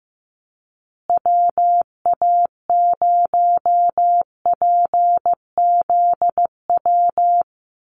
Morse code, heard as a keyed tone, about 15 wpm, 700 Hz. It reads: WA0PZW
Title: USA Call Signs Repeated 3x 15wpm